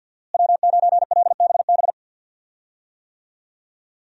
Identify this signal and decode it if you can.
Morse code, heard as a keyed tone, about 52 wpm, 700 Hz. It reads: W9LBB
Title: USA Call Signs - 5 Characters - Repeated 3x - 50wpm